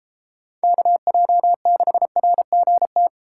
Morse code, heard as a keyed tone, about 30 wpm, 700 Hz. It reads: KJ6RGT